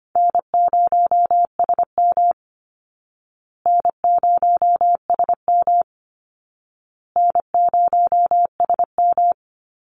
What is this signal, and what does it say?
Morse code, heard as a keyed tone, about 25 wpm, 700 Hz. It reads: N0SM N0SM N0SM
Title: Call Signs Repeated 2x - 25wpm